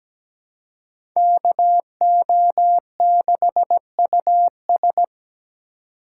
Morse code, heard as a keyed tone, about 17 wpm, 700 Hz. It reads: KO6US
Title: USA Call Signs - 5 Characters - 3x Character Spacing - 17wpm